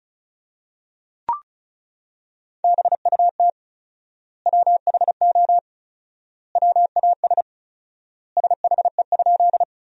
Morse code, heard as a keyed tone, about 36 wpm, 700 Hz. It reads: BUT WHO WAS SHE?
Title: 4x Word-Spacing - Sentences from Top 100 Words - 35wpm